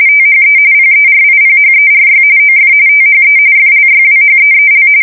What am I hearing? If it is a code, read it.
RTTY, about 45.45 baud. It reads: THE PERIPHERY OF THE FLYWHEEL
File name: fsk170 / morkrum